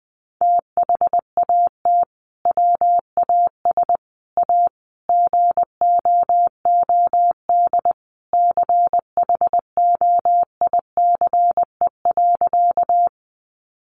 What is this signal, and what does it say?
Morse code, heard as a keyed tone, about 20 wpm, 700 Hz. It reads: THAT WAS A GOOD CHOICE.
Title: Sentences from Top 900 Words 20wpm